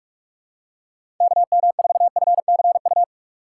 Morse code, heard as a keyed tone, about 42 wpm, 700 Hz. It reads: KM4FCU
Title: USA Call Signs - 6 Characters - 4x Character Spacing - 45wpm